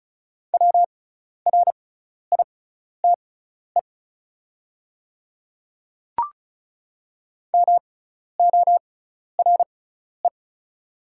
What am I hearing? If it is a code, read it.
Morse code, heard as a keyed tone, about 34 wpm, spaced as for 12 wpm, 700 Hz. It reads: WRITE MORE
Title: 6x Character Spacing - Top 200 Words - 35wpm